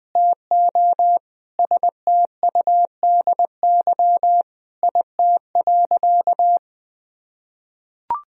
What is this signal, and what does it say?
Morse code, heard as a keyed tone, about 21 wpm, 700 Hz. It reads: TO STUDY IT.
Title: Sentences from Top 1000 Words - Review Part 1 - 20wpm